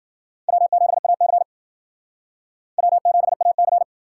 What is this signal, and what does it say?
Morse code, heard as a keyed tone, about 60 wpm, 700 Hz. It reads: W7AC W7AC